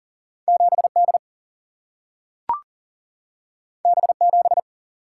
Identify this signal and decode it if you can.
Morse code, heard as a keyed tone, about 39 wpm, 700 Hz. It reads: 7D B7
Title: Single-Letter Single-Number 40wpm